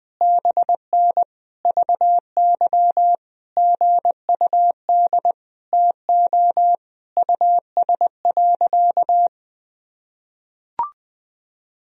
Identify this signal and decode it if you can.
Morse code, heard as a keyed tone, about 20 wpm, 700 Hz. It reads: BN VY GUD TO US.
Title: Sentences from Top 200 Words and Abbreviations 20wpm